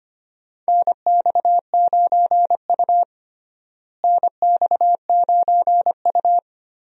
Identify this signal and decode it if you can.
Morse code, heard as a keyed tone, about 25 wpm, 700 Hz. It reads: NX9U NX9U
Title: USA Call Signs - 4 Characters - Repeated 3x - 25wpm